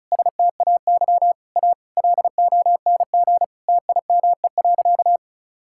Morse code, heard as a keyed tone, about 36 wpm, 700 Hz. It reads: STAY A LONG TIME.
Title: Sentences from Top 1000 Words - Review Part 6 - 35wpm